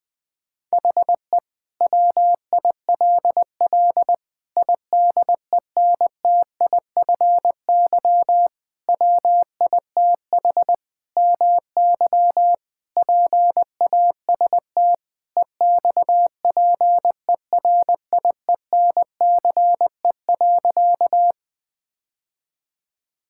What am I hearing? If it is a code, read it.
Morse code, heard as a keyed tone, about 20 wpm, 700 Hz. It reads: HE WILL IDENTIFY WITH MY PAST EXPERIENCE.
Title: Sentences from Top 1000 Words - Review Part 6 - 20wpm